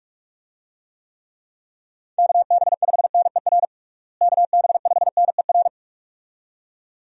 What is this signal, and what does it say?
Morse code, heard as a keyed tone, about 46 wpm, 700 Hz. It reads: KB5NER KB5NER
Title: USA Call Signs - 6 Characters - Repeated 2x - 45wpm